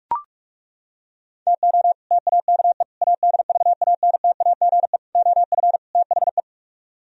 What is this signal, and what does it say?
Morse code, heard as a keyed tone, about 39 wpm, 700 Hz. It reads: TO TAKE ADVANTAGE OF THE